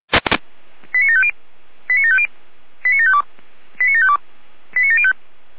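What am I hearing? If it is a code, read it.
Five-tone selective call (ZVEI), heard as quick runs of five tones, tones of about 70 ms. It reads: 89750 89750 89752 89752 89705
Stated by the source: ZVEI five-tone selective call received on an SDR.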